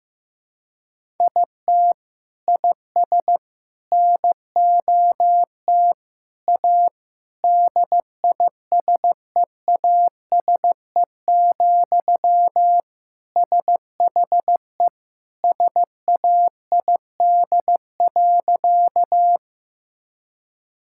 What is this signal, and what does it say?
Morse code, heard as a keyed tone, about 16 wpm, 700 Hz. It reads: IT IS NOT A DISEASE, SHE SAID.
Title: Sentences from Top 1000 Words - Review Part 6 - 15wpm